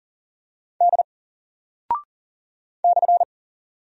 Morse code, heard as a keyed tone, about 40 wpm, 700 Hz. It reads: D /
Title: Single Character (Letter:Number:Punctuation:Prosign) 40wpm